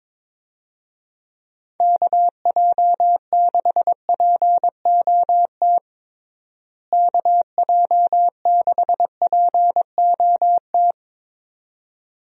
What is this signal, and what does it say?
Morse code, heard as a keyed tone, about 22 wpm, 700 Hz. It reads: KJ6POT KJ6POT
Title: USA Call Signs - 6 Characters - Repeated 2x - 22wpm